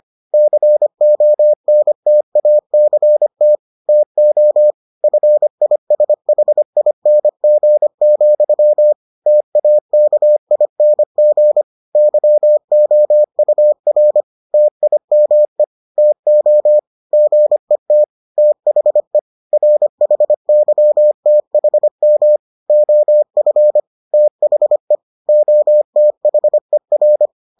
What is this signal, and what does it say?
Morse code, heard as a keyed tone, about 25 wpm, 600 Hz. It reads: CONTACT TO FISHING, TAKING YOUR TIME TO GET THE RHYTHM OF THE OTHER